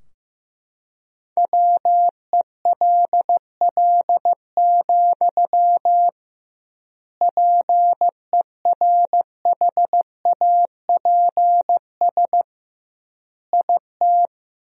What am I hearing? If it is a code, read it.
Morse code, heard as a keyed tone, about 15 wpm, 700 Hz. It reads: WELL, PERHAPS IT
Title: 2x Word-Spacing - Sentences from Top 500 Words - 15wpm